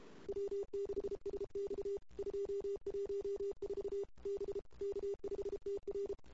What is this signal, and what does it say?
Morse code, heard as a keyed tone, about 29 wpm, 410 Hz. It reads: W6SX 214 B K5TR
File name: CW / 2008-ss-cw-k5tr-op-n5rz -22:00